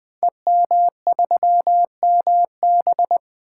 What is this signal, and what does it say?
Morse code, heard as a keyed tone, about 20 wpm, 700 Hz. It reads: EM3MB
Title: Code Groups - 5 Letter:Numbers - 20wpm